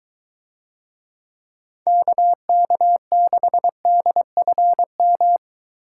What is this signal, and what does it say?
Morse code, heard as a keyed tone, about 22 wpm, 700 Hz. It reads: KK6DFM